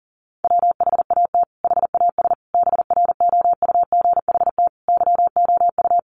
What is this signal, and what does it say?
Morse code, heard as a keyed tone, about 40 wpm, 700 Hz. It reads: WHAT HAS BROUGHT YOU
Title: Sentences from Top 500 Words 40wpm